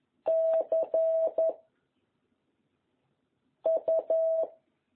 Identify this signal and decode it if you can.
Morse code, heard as a keyed tone, about 11 wpm, 660 Hz. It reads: C U
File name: jh8cbh / 2021cw